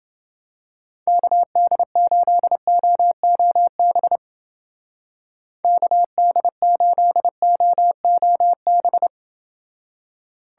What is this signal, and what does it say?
Morse code, heard as a keyed tone, about 30 wpm, 700 Hz. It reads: KD8OOB KD8OOB